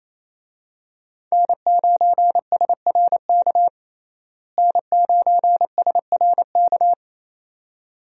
Morse code, heard as a keyed tone, about 28 wpm, 700 Hz. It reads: N9SRK N9SRK
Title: USA Call Signs - 5 Characters - Repeated 2x - 28wpm